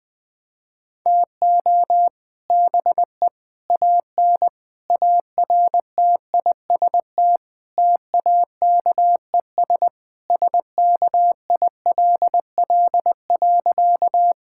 Morse code, heard as a keyed tone, about 20 wpm, 700 Hz. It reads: TO BE AN ARTIST TAKES SKILL.
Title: Sentences with 3-Word Phrases - Spoken Beginning - 20wpm